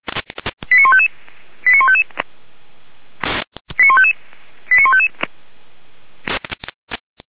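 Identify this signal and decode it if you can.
Five-tone selective call (ZVEI), heard as quick runs of five tones, tones of about 70 ms. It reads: 89155 89155 89155 89155
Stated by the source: ZVEI five-tone selective call 89494 received by SDR.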